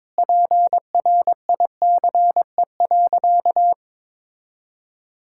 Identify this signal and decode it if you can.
Morse code, heard as a keyed tone, about 22 wpm, 700 Hz. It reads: PRICE.
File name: Sentences with 7-Word Phrases - 22wpm